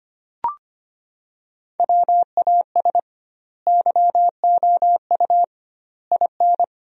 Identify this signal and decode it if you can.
Morse code, heard as a keyed tone, about 25 wpm, 700 Hz. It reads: WAS YOU IN
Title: 2x Word-Spacing - Sentences from Top 300 Words - 25wpm